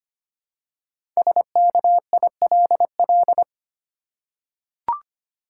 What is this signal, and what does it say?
Morse code, heard as a keyed tone, about 25 wpm, 700 Hz. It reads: SKILL
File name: 5 Letter Words - First 3 Letters Given - 25wpm